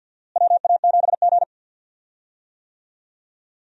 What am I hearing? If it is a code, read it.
Morse code, heard as a keyed tone, about 46 wpm, 700 Hz. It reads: WA7G
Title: USA Call Signs - 4 Characters - Repeated 2x - 50wpm